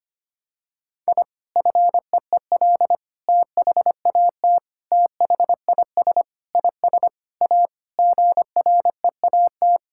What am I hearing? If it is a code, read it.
Morse code, heard as a keyed tone, about 25 wpm, 700 Hz. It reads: I FEEL THAT THIS IS A GREAT
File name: practice / 25wpm / Sentences with 4-Word Phrases - 25wpm